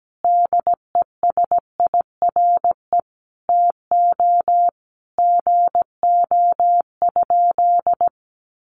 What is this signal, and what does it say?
Morse code, heard as a keyed tone, about 17 wpm, 700 Hz. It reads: DESIRE TO GO?
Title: Sentences from Top 1100 Words 17wpm